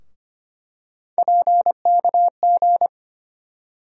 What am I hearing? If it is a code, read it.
Morse code, heard as a keyed tone, about 25 wpm, 700 Hz. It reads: PKG
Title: USA Call Sign Postfixes - 3 Letters 25wpm